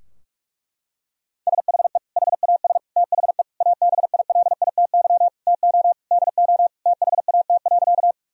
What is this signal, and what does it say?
Morse code, heard as a keyed tone, about 42 wpm, 700 Hz. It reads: SHE HAS THE ABILITY TO DO THAT.
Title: Sentences from Top 1000 Words - Review Part 3 - 45wpm